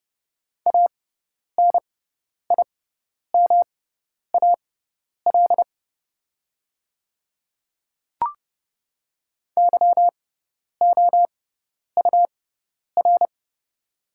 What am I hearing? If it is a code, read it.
Morse code, heard as a keyed tone, about 30 wpm, spaced as for 10 wpm, 700 Hz. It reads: ANIMAL YOUR
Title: 6x Character Spacing - Top 200 Words - 30wpm